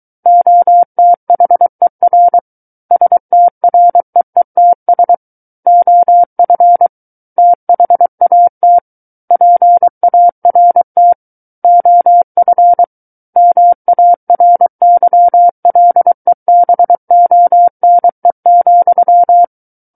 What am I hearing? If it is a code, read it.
Morse code, heard as a keyed tone, about 23 wpm, 700 Hz. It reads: OTHER STREETS OF THAT PART OF MARYLEBONE,
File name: Book One / 14-Chapter 14 - 23 WPM